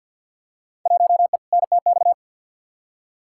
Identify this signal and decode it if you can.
Morse code, heard as a keyed tone, about 47 wpm, 700 Hz. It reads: 1E NTX
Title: Field Day Exchanges - 50wpm